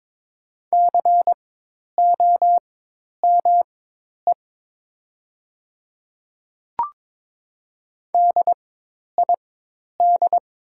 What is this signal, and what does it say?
Morse code, heard as a keyed tone, about 22 wpm, spaced as for 10 wpm, 700 Hz. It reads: COME DID
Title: 4x Character Spacing - Top 100 Words - 22wpm